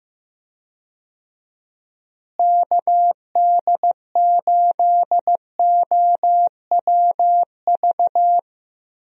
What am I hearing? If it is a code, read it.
Morse code, heard as a keyed tone, about 15 wpm, 700 Hz. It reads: KD8OWV
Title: USA Call Signs - 6 Characters - Repeated 2x - 15wpm